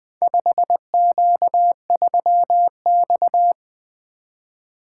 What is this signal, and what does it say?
Morse code, heard as a keyed tone, about 20 wpm, 700 Hz. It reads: 5Q3X
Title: Four Letter-Number Combinations - 20wpm